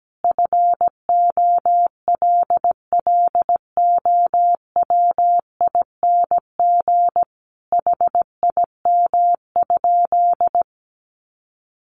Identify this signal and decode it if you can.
Morse code, heard as a keyed tone, about 17 wpm, 700 Hz. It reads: FOLLOWING HIM?
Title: Sentences from Top 700 Words 17wpm